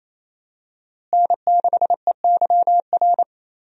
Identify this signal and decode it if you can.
Morse code, heard as a keyed tone, about 27 wpm, 700 Hz. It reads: N6EYR
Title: USA Call Signs - 5 Characters - Repeated 3x - 28wpm